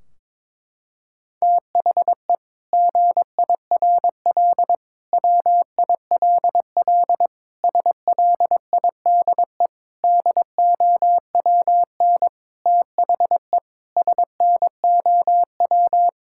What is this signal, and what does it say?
Morse code, heard as a keyed tone, about 22 wpm, 700 Hz. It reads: THE GIRL WILL SLIDE DOWN THE SNOW
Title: Sentences from Top 1600 Words 22wpm